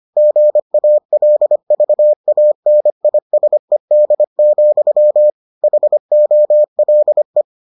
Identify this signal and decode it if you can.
Morse code, heard as a keyed tone, about 25 wpm, 600 Hz. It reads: GALVANISED, HOLE